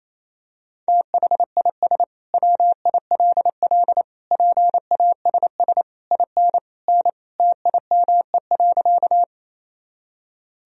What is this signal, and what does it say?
Morse code, heard as a keyed tone, about 28 wpm, 700 Hz. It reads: THIS WILL PASS IN N TIME.
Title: Sentences from Top 400 Words and Abbreviations 28wpm